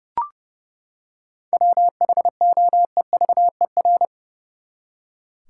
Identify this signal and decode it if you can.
Morse code, heard as a keyed tone, about 30 wpm, 700 Hz. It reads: WHOEVER
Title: Seven Letter Words - 30wpm